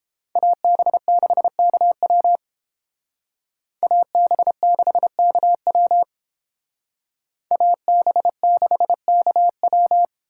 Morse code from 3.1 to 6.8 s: AB6KW

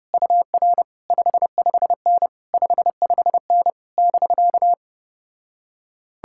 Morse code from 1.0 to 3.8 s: 55N 55N

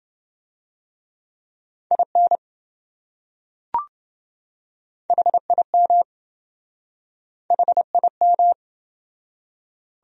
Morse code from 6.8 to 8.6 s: HIM

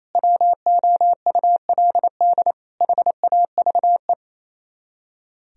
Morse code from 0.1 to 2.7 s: WOULD

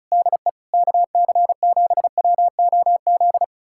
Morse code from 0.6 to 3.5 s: KC7WOZ

